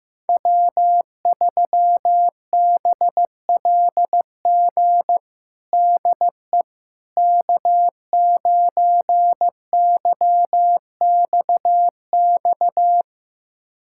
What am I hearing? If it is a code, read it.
Morse code, heard as a keyed tone, about 15 wpm, 700 Hz. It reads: W3BLG DE K9YXX
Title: USA Callsign DE Callsign - 5 Characters - 15wpm